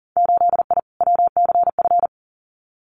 Morse code, heard as a keyed tone, about 40 wpm, 700 Hz. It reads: 8I WCF
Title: Winter Field Day Exchanges Encore 40wpm